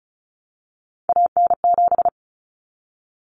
Morse code, heard as a keyed tone, about 34 wpm, 700 Hz. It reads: AN7
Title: Common Cut Numbers - 3-Digit 35wpm